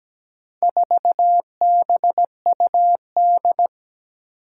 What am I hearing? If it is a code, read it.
Morse code, heard as a keyed tone, about 17 wpm, 700 Hz. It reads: 4BUD
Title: Four Challenging Letter-Number Combinations 17wpm